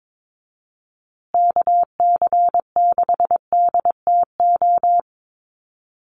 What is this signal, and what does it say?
Morse code, heard as a keyed tone, about 22 wpm, 700 Hz. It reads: KC6DTO
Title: USA Call Signs Repeated 3x 22wpm